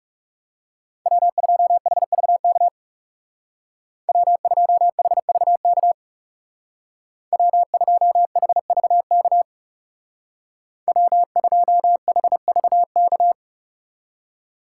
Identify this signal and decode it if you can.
Morse code, heard as a keyed tone, about 36 wpm, 700 Hz. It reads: W2HVK W2HVK W2HVK W2HVK
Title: Speed Racing - USA 1x3 Call Signs 30wpm